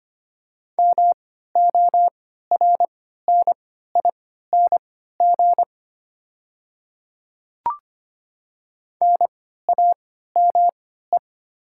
Morse code, heard as a keyed tone, about 25 wpm, spaced as for 14 wpm, 690 Hz. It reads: MORNING NAME
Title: 3x Character Spacing - Top 500 Words - 25wpm